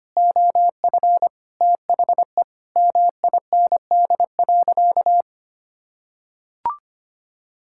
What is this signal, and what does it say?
Morse code, heard as a keyed tone, about 25 wpm, 700 Hz. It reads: OF THE MIND.
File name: practice / 25wpm / Sentences with 4-Word Phrases - 25wpm